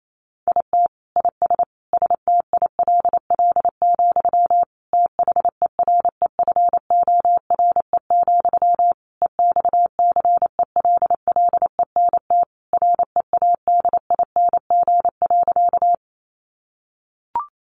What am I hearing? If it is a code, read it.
Morse code, heard as a keyed tone, about 28 wpm, 700 Hz. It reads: IT IS STILL, THEREFORE, EXCELLENT READING.
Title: Sentences from Top 1000 Words 28wpm